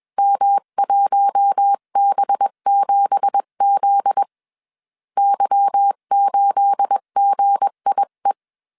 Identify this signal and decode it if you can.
Morse code, heard as a keyed tone, about 21 wpm, 800 Hz. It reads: M1B7Z Y8GIE
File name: CW / DET / 1 9 1500C S B 20wpm 2313